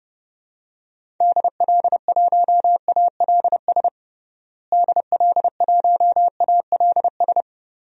Morse code, heard as a keyed tone, about 30 wpm, 700 Hz. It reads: DL1ALS DL1ALS
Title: DXCC-100 Call Signs - 6 Characters - Repeated 3x - 30wpm